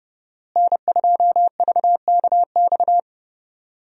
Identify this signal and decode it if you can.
Morse code, heard as a keyed tone, about 32 wpm, 700 Hz. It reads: N2VKX
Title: USA Call Signs - 5 Characters - Repeated 2x - 30wpm